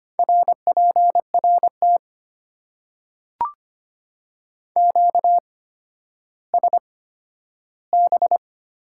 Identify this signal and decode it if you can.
Morse code, heard as a keyed tone, about 25 wpm, spaced as for 12 wpm, 690 Hz. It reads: RPRT QSB
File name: Top 90 QSO Elements - 8x Character Spacing - 25wpm